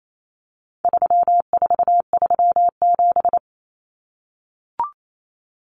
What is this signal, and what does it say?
Morse code, heard as a keyed tone, about 28 wpm, 700 Hz. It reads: 3437
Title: Common Cut Numbers - 4-Digit 28wpm